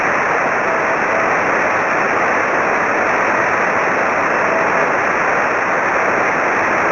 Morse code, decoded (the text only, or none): none